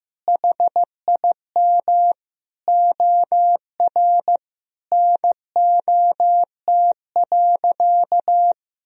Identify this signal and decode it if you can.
Morse code, heard as a keyed tone, about 15 wpm, 700 Hz. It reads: HIM OR NOT.
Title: Sentences from Top 1000 Words - Review Part 3 - 15wpm